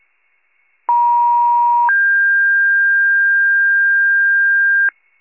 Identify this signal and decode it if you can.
Two-tone page, heard as a one-second tone, then a three-second tone, about 950 then 1600 Hz, with noise. It is heard from 0.9 to 4.9 s.